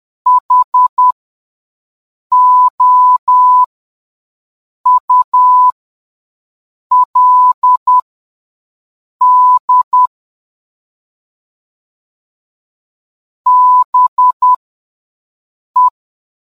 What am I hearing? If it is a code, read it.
Morse code, heard as a keyed tone, about 10 wpm, spaced as for 5 wpm, 1000 Hz. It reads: HOULD BE